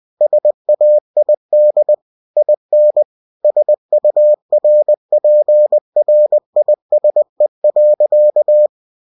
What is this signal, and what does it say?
Morse code, heard as a keyed tone, about 20 wpm, 600 Hz. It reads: SAID IN SURPRISE.
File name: cw-books / winnie-the-pooh / winnie-20wpm-page-0026